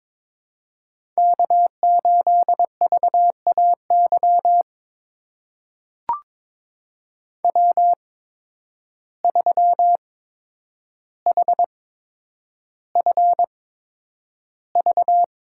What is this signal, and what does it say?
Morse code, heard as a keyed tone, about 22 wpm, spaced as for 9 wpm, 700 Hz. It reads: K8VAY W3HFV
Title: USA Call Signs - 5 Characters - 8x Character Spacing - 22wpm